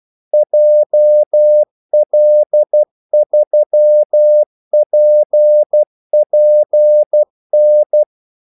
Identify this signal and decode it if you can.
Morse code, heard as a keyed tone, about 12 wpm, 600 Hz. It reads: JL3PPN
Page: 2 ekstra ordmellemrum mellem kaldesignalerne